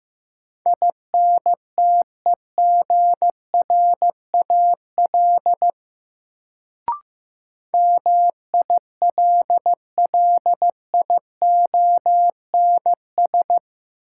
Morse code, heard as a keyed tone, about 15 wpm, 700 Hz. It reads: INTEGRAL MILLIONS
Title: Sets of 5 Words - New Word Drill - 15wpm